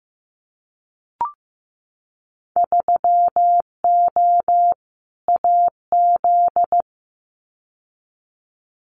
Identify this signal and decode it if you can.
Morse code, heard as a keyed tone, about 15 wpm, 700 Hz. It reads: 3O AZ
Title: Winter Field Day Exchanges 15wpm